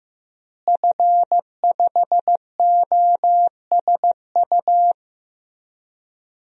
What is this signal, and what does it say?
Morse code, heard as a keyed tone, about 15 wpm, 700 Hz. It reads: F5OSU
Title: DXCC-100 Call Signs - 5 Characters - Repeated 2x - 15wpm